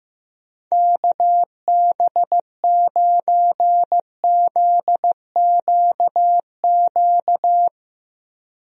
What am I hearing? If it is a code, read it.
Morse code, heard as a keyed tone, about 15 wpm, 700 Hz. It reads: KB9ZQQ
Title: USA Call Signs - 6 Characters - Repeated 2x - 15wpm